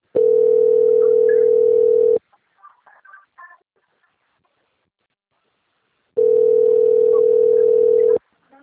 Ringback tone (North America), 440 + 480 Hz.
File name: out- - - - - .44128